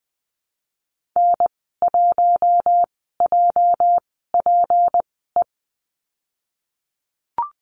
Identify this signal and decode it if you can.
Morse code, heard as a keyed tone, about 20 wpm, spaced as for 15 wpm, 700 Hz. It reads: N1JPE
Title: 238 Lesson 38g - Letter q - Call Signs 20wpm